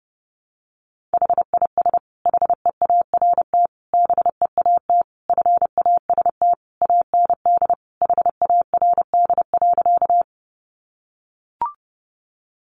Morse code, heard as a keyed tone, about 30 wpm, 700 Hz. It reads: HIS HEART BEAT FAST AND HARD.
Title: Sentences from Top 1300 Words 30wpm